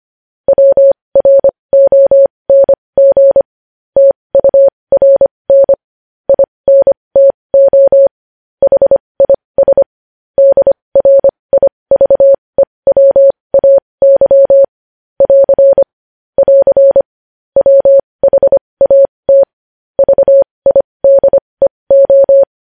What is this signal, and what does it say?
Morse code, heard as a keyed tone, about 25 wpm, spaced as for 20 wpm, 550 Hz. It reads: WRONG TURN INTO HIS DRIVEWAY <AR> <AR> WHAT VIDEO
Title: News Headlines at 20